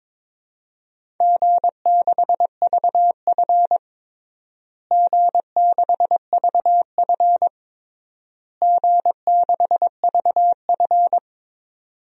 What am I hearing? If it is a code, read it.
Morse code, heard as a keyed tone, about 22 wpm, 700 Hz. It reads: G6VF G6VF G6VF
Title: DXCC-100 Call Signs - 4 Characters - Repeated 3x - 22wpm